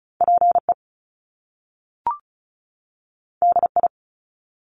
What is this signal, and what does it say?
Morse code, heard as a keyed tone, about 36 wpm, 700 Hz. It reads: PE DI
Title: Top Two Letter Combinations in Words 35wpm